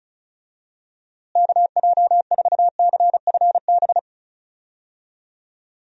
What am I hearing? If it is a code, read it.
Morse code, heard as a keyed tone, about 36 wpm, 700 Hz. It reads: KJ4CFB
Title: USA Call Signs - 6 Characters - Repeated 2x - 35wpm